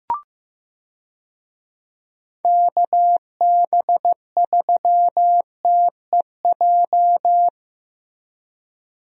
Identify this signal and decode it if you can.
Morse code, heard as a keyed tone, about 15 wpm, 700 Hz. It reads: KB3TEJ
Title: USA Call Signs - 6 Characters - Repeated 2x - 15wpm